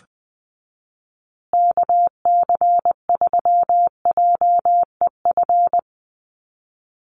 Morse code, heard as a keyed tone, about 20 wpm, 700 Hz. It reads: KC3JEF
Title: USA Call Signs Repeated 3x 20wpm